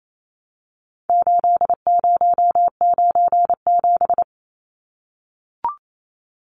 Morse code, heard as a keyed tone, about 28 wpm, 700 Hz. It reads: 8097
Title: Numbers - 4-Digit 28wpm